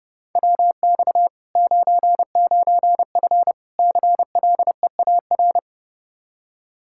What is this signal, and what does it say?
Morse code, heard as a keyed tone, about 29 wpm, 700 Hz. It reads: WX 99F CLEAR
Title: Practice QSO - Exchange 2 - Info - 30wpm